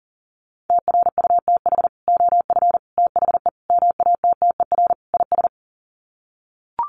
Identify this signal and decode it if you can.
Morse code, heard as a keyed tone, about 40 wpm, 700 Hz. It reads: TRUTH OF THE MATTER IS
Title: Sets of 5 Words Encore2 40wpm